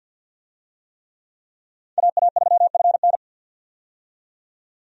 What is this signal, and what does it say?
Morse code, heard as a keyed tone, about 52 wpm, 700 Hz. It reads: AA3FN
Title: USA Call Signs - 5 Characters - 50wpm